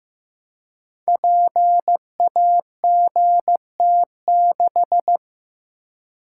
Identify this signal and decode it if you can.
Morse code, heard as a keyed tone, about 15 wpm, 700 Hz. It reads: PAGT6
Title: Code Groups - 5 Letter:Numbers - 15wpm